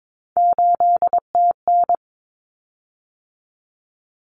Morse code, heard as a keyed tone, about 22 wpm, 700 Hz. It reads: 8TN
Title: Common Cut Numbers - 3-Digit 22wpm